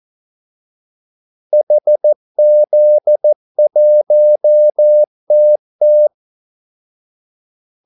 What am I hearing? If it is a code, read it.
Morse code, heard as a keyed tone, about 14 wpm, 600 Hz. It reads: HZ1TT
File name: Callsigns-14wpm